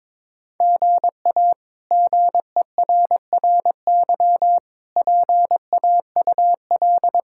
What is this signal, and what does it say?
Morse code, heard as a keyed tone, about 21 wpm, 700 Hz. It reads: GA GERRY PAUL
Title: K1USN SST QSO Practice - 22wpm